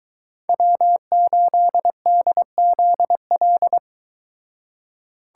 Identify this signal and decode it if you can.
Morse code, heard as a keyed tone, about 23 wpm, 700 Hz. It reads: W8DZL